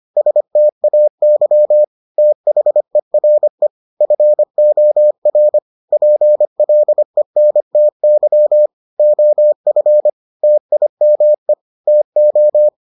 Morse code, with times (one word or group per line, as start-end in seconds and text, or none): 0.2-1.8 STAY
2.2-3.7 THERE
4.0-5.6 FOR
5.9-8.7 PLENTY
9.0-10.1 OF
10.4-11.5 TIME
11.9-12.7 TO